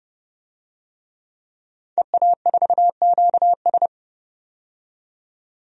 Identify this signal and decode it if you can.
Morse code, heard as a keyed tone, about 30 wpm, 700 Hz. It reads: EA4QS
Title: DXCC-100 Call Signs - 5 Characters - 30wpm